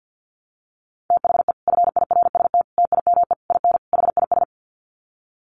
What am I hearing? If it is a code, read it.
Morse code, heard as a keyed tone, about 48 wpm, 700 Hz. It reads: THE FIRST TIME IN HIS
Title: Sets of 5 Words Encore2 50wpm